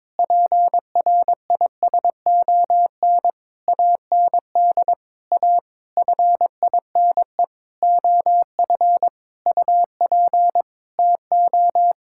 Morse code, heard as a keyed tone, about 22 wpm, 700 Hz. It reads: PRISON AND A FINE OF UP TO
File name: Sentences with 9-Word Phrases - 22wpm